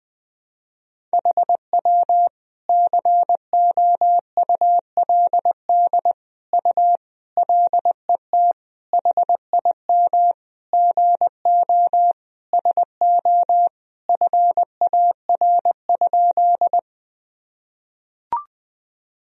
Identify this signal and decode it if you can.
Morse code, heard as a keyed tone, about 21 wpm, 700 Hz. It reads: HW COULD U LET HIM GO SO FAR?
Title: Sentences from Top 300 Words and Abbreviations 20wpm